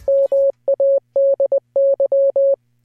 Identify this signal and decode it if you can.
Morse code, heard as a keyed tone, about 20 wpm, 560 Hz. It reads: MADY